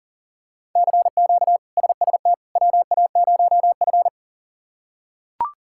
Morse code, heard as a keyed tone, about 37 wpm, 690 Hz. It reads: CQ SST WA0F
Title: K1USN SST QSO Practice - 40wpm